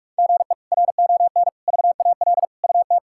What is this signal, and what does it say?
Morse code, heard as a keyed tone, about 44 wpm, 700 Hz. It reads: GE RON VAL UT
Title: K1USN SST QSO Practice - 45wpm